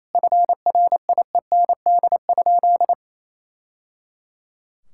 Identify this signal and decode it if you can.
Morse code, heard as a keyed tone, about 28 wpm, 700 Hz. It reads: FRIEND?
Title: Sentences from Top 1000 Words - Review Part 6 - 28wpm